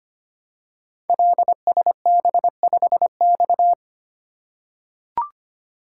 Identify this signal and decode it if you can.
Morse code, heard as a keyed tone, about 25 wpm, 700 Hz. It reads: LSB5X
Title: Code Groups - 5 Letter:Numbers - 25wpm